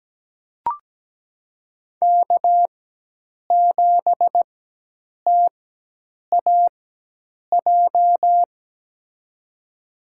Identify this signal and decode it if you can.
Morse code, heard as a keyed tone, about 17 wpm, spaced as for 8 wpm, 700 Hz. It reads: K7TAJ
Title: USA Call Signs - 5 Characters - 4x Character Spacing - 17wpm